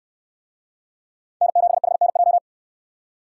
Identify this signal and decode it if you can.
Morse code, heard as a keyed tone, about 69 wpm, 700 Hz. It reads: N7HNJ